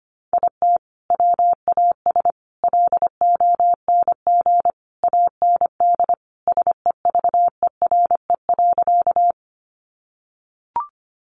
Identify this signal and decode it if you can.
Morse code, heard as a keyed tone, about 25 wpm, 700 Hz. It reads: IT WAS LONG AND SEVERE.
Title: Sentences from Top 1400 Words 25wpm